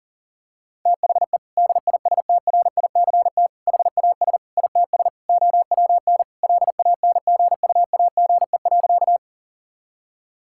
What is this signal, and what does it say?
Morse code, heard as a keyed tone, about 40 wpm, 700 Hz. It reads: THE DISTRICT HAS ITS OWN LANGUAGE.
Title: Sentences from Top 1600 Words 40wpm